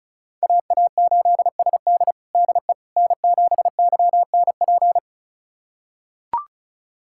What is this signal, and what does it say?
Morse code, heard as a keyed tone, about 36 wpm, 700 Hz. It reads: AA8SD DE N7YNP
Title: USA Callsign DE Callsign - 5 Characters - 35wpm